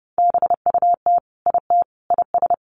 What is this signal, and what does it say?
Morse code, heard as a keyed tone, about 28 wpm, 700 Hz. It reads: BUT IT IS
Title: Sentences from Top 1100 Words 30wpm